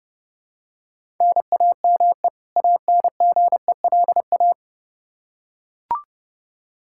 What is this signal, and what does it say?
Morse code, heard as a keyed tone, about 30 wpm, 700 Hz. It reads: NAME ANGELA
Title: K1USN SST QSO Practice 30wpm